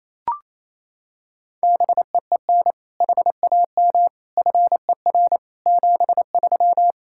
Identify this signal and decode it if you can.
Morse code, heard as a keyed tone, about 28 wpm, 700 Hz. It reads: BEEN HAM FER 73
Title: Practice QSO - Exchange 3 - More Info - 28wpm